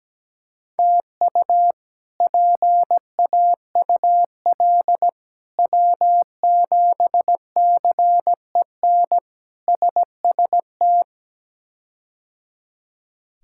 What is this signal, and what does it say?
Morse code, heard as a keyed tone, about 17 wpm, 700 Hz. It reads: TU PAUL W7CEN SST
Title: K1USN SST QSO Practice - 17wpm